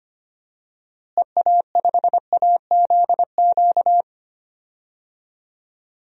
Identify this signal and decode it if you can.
Morse code, heard as a keyed tone, about 25 wpm, 700 Hz. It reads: EA5AZQ
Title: DXCC-100 Call Signs - 6 Characters - Repeated 3x - 25wpm